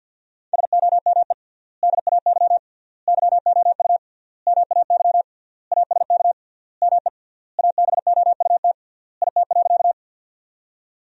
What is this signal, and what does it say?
Morse code, heard as a keyed tone, about 50 wpm, 700 Hz. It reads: SOME DAY YOU MAY ASK ME ABOUT IT.
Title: 3x Word-Spacing - Sentences from Top 200 Words - 50wpm